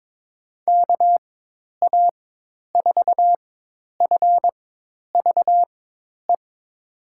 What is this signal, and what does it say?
Morse code, heard as a keyed tone, about 22 wpm, spaced as for 10 wpm, 700 Hz. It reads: KA4FVE
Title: USA Call Signs - 6 Characters - 4x Character Spacing - 22wpm